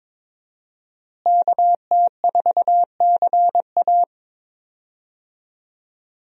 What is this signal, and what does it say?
Morse code, heard as a keyed tone, about 22 wpm, 700 Hz. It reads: KT4CA
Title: USA Call Signs - 5 Characters - Repeated 2x - 22wpm